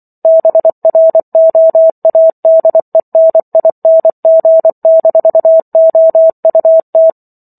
Morse code, heard as a keyed tone, about 24 wpm, 650 Hz. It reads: BROADENING-OUT